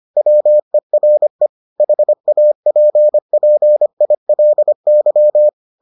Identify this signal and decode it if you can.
Morse code, heard as a keyed tone, about 25 wpm, 600 Hz. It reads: WERE HAPPILY